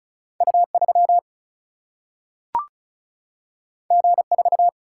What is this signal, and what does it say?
Morse code, heard as a keyed tone, about 36 wpm, 700 Hz. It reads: U 3 G 4
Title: Single-Letter Single-Number 35wpm